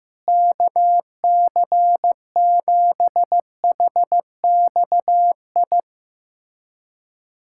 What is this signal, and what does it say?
Morse code, heard as a keyed tone, about 15 wpm, 700 Hz. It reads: KC7HXI
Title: USA Callsign DE Callsign - 6 Characters - 15wpm